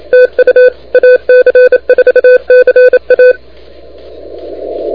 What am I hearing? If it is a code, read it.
Morse code, heard as a keyed tone, about 29 wpm, 500 Hz. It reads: TU AC4CA
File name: recordings / AC4CA de NT1K ARRL10M CW 28MHz